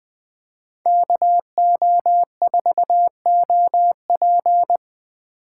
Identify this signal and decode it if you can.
Morse code, heard as a keyed tone, about 20 wpm, 700 Hz. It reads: KO4OP